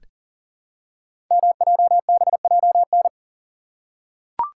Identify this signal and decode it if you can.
Morse code, heard as a keyed tone, about 40 wpm, 690 Hz. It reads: MJBJN
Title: Code Groups - 5 Letters - 40wpm